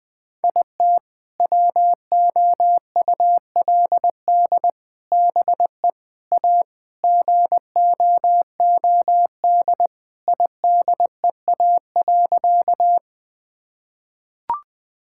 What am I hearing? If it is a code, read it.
Morse code, heard as a keyed tone, about 19 wpm, 700 Hz. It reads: IT WOULD BE A GOOD IDEA.
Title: Sentences with 3-Word Phrases - 20wpm